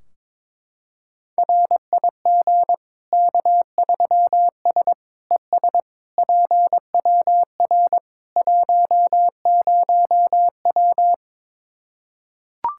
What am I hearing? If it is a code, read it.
Morse code, heard as a keyed tone, about 21 wpm, 700 Hz. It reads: RIG K3S ES PWR 10W
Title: Practice QSO - Exchange 2 - Info - 22wpm